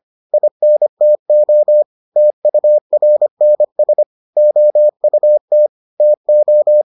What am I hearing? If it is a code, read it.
Morse code, heard as a keyed tone, about 25 wpm, 600 Hz. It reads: INTO TURNS OUT TO